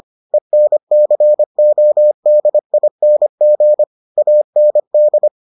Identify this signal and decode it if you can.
Morse code, heard as a keyed tone, about 25 wpm, 600 Hz. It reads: ENCODING AND